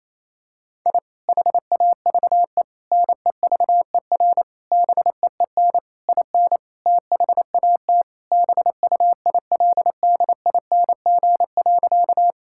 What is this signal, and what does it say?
Morse code, heard as a keyed tone, about 28 wpm, 700 Hz. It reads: I HAVE NEVER BEEN IN THAT BUILDING.
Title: Sentences from Top 1000 Words - Review Part 5 - 28wpm